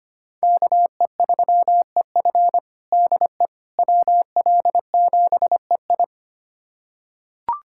Morse code, heard as a keyed tone, about 25 wpm, 700 Hz. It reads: KE3EF DE WL7EI
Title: USA Callsign DE Callsign - 5 Characters - 25wpm